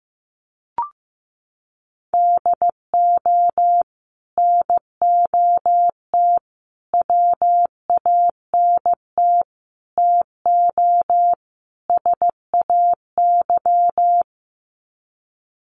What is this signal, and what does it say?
Morse code, heard as a keyed tone, about 15 wpm, 700 Hz. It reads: DO NOT WANT TO SAY
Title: Sets of 5 Words Encore 15wpm